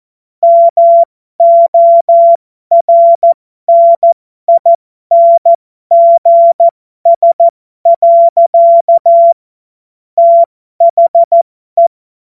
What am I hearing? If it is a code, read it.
Morse code, heard as a keyed tone, about 14 wpm, spaced as for 12 wpm, 680 Hz. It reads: MORNINGS. THE
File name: Morsekurs / Witze